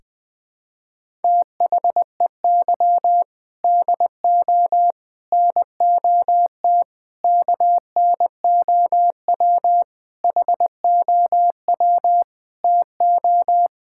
Morse code, heard as a keyed tone, about 20 wpm, 700 Hz. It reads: THEY DO NOT KNOW HOW TO
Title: Sentences with 5-Word Phrases - Spoken Beginning - 20wpm